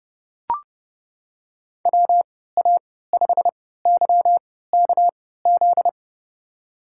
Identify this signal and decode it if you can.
Morse code, heard as a keyed tone, about 30 wpm, spaced as for 17 wpm, 700 Hz. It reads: WA5YKZ
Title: USA Call Signs - 6 Characters - 3x Character Spacing - 30wpm